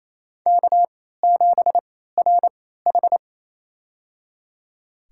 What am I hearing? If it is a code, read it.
Morse code, heard as a keyed tone, about 28 wpm, spaced as for 16 wpm, 700 Hz. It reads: K7RH
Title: USA Call Signs - 4 Characters - 3x Character Spacing - 28wpm